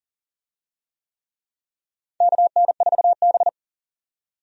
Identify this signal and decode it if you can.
Morse code, heard as a keyed tone, about 40 wpm, 700 Hz. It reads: KN4B
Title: USA Call Signs - 4 Characters - 40wpm